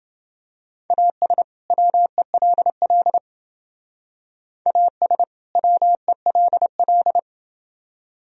Morse code, heard as a keyed tone, about 28 wpm, 700 Hz. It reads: AS WELL AS WELL